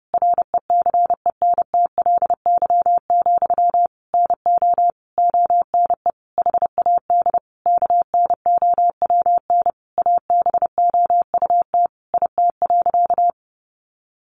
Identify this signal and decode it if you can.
Morse code, heard as a keyed tone, about 30 wpm, 700 Hz. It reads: RECENTLY, NO ONE HAD KNOWN ABOUT IT.
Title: Sentences from Top 1200 Words 30wpm